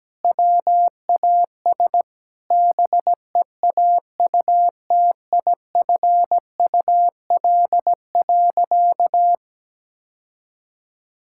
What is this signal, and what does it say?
Morse code, heard as a keyed tone, about 17 wpm, 700 Hz. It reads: WAS BEAUTIFUL.
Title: Sentences from Top 1000 Words - Review Part 5 - 17wpm